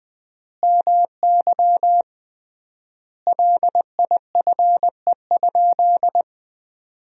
Morse code, heard as a keyed tone, about 20 wpm, 700 Hz. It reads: MY LIFE?
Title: 3x Word-Spacing - Sentences from Top 300 Words - 20wpm